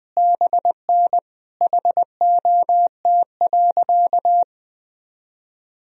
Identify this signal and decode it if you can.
Morse code, heard as a keyed tone, about 21 wpm, 700 Hz. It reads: BN HOT.
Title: Sentences from Top 500 Words and Abbreviations 20wpm